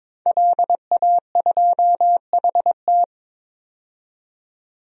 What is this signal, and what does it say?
Morse code, heard as a keyed tone, about 21 wpm, 700 Hz. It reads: LA2HT
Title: DXCC-100 Call Signs - 5 Characters - 22wpm